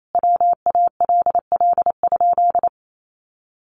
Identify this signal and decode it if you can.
Morse code, heard as a keyed tone, about 28 wpm, 700 Hz. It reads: WALL?
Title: Sentences from Top 1100 Words 28wpm